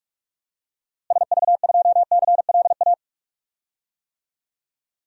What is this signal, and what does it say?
Morse code, heard as a keyed tone, about 46 wpm, 700 Hz. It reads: SV2CLA
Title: DXCC-100 Call Signs - 6 Characters - Repeated 2x - 45wpm